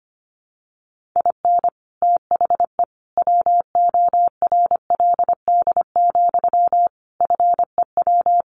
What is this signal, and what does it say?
Morse code, heard as a keyed tone, about 25 wpm, 700 Hz. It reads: IN THE WORLD, FEW
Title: Sentences from Top 1400 Words 25wpm